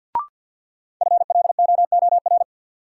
Morse code, heard as a keyed tone, about 52 wpm, 700 Hz. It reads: FLOOR
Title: Sets of 3 Words - New Word Drill - 50wpm